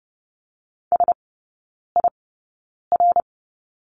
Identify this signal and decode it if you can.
Morse code, heard as a keyed tone, about 30 wpm, spaced as for 9 wpm, 690 Hz. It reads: SIR
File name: Lesson 16e - Letter c - Three Characters 30wpm